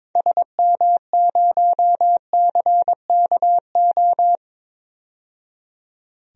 Morse code, heard as a keyed tone, about 22 wpm, 700 Hz. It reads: SM0CKO
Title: DXCC-100 Call Signs - 6 Characters - 22wpm